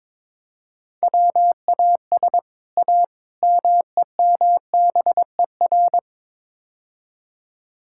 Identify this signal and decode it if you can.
Morse code, heard as a keyed tone, about 21 wpm, 700 Hz. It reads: WAS A MEMBER
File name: practice / 22wpm / Build A Phrase - 5-Word Phrases - 22wpm